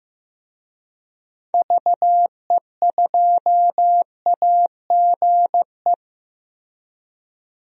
Morse code, heard as a keyed tone, about 15 wpm, 700 Hz. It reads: VE2AGE
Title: DXCC-100 Call Signs - 6 Characters - Repeated 3x - 15wpm